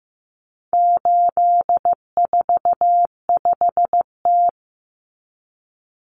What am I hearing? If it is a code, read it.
Morse code, heard as a keyed tone, about 15 wpm, 700 Hz. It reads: 845T
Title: Common Cut Numbers - 4-Digit 15wpm